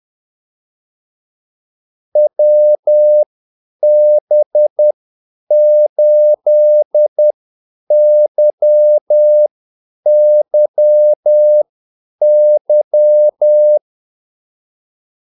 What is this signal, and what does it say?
Morse code, heard as a keyed tone, about 10 wpm, spaced as for 8 wpm, 600 Hz. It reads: WB8YYY
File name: Callsigns-10-8wpm